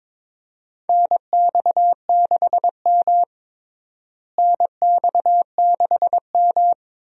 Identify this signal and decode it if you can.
Morse code, heard as a keyed tone, about 22 wpm, 700 Hz. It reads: NX6M NX6M
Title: USA Call Signs - 4 Characters - Repeated 2x - 22wpm